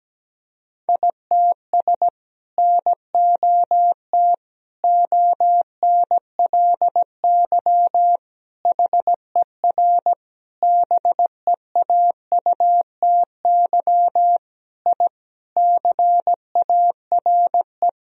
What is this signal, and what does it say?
Morse code, heard as a keyed tone, about 17 wpm, 700 Hz. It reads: ITS NOT ONLY HER BEAUTY I CARE
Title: Sentences from Top 1000 Words - Review Part 2 - 17wpm